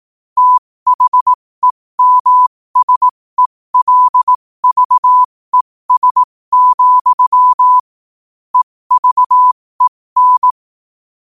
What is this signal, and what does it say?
Morse code, heard as a keyed tone, about 18 wpm, spaced as for 15 wpm, 1000 Hz. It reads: THEMSELVES, EVEN